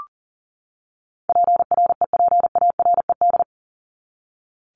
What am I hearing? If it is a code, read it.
Morse code, heard as a keyed tone, about 38 wpm, 700 Hz. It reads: PREPARED
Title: ED Suffix - Top 100 Words with Base Words 40wpm